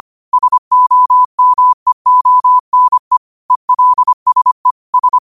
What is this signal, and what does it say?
Morse code, heard as a keyed tone, about 25 wpm, 1000 Hz. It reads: SOMEONE ELSE S
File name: qotdfiles / 25wpm